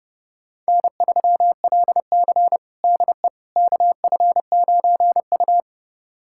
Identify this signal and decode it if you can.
Morse code, heard as a keyed tone, about 32 wpm, 700 Hz. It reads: N3LC DE KF9U
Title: USA Callsign DE Callsign - 4 Characters - 30wpm